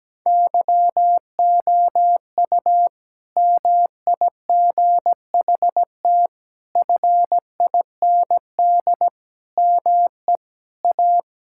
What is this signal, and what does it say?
Morse code, heard as a keyed tone, about 17 wpm, 700 Hz. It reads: YOU MIGHT FIND ME A
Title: Sentences from Top 1600 Words 17wpm